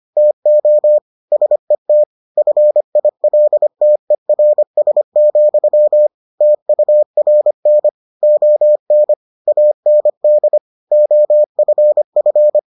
Morse code, heard as a keyed tone, about 25 wpm, 600 Hz. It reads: TO SET FILTERS, TURN ON AND OFF